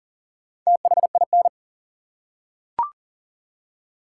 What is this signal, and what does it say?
Morse code, heard as a keyed tone, about 40 wpm, 700 Hz. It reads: THIN
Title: Four Letter Words - 40wpm